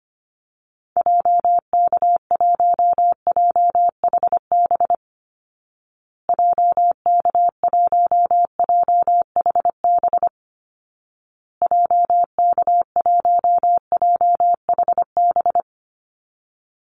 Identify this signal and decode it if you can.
Morse code, heard as a keyed tone, about 25 wpm, 700 Hz. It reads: JK1JHB JK1JHB JK1JHB
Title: Call Signs Repeated 2x - 25wpm